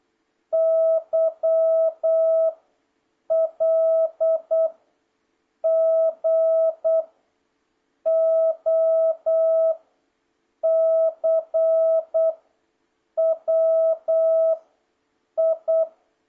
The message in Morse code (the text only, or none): Y L G O C W I